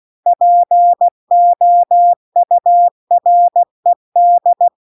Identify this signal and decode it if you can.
Morse code, heard as a keyed tone, about 16 wpm, 700 Hz. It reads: POURED
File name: CW-The-Big-Twoheaded-River-2-16WPM